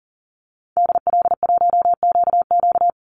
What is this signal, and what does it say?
Morse code, heard as a keyed tone, about 40 wpm, 700 Hz. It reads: DL1QQ
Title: Call Signs Repeated 2x - 40wpm